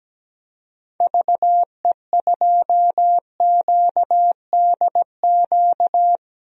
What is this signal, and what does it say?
Morse code, heard as a keyed tone, about 17 wpm, 700 Hz. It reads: VE2QDQ
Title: DXCC-100 Call Signs - 6 Characters - Repeated 2x - 17wpm